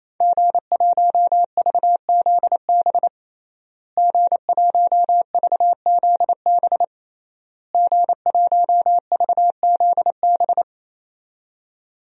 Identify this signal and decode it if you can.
Morse code, heard as a keyed tone, about 27 wpm, 700 Hz. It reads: G1VZB G1VZB G1VZB
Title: DXCC-100 Call Signs - 5 Characters - Repeated 3x - 28wpm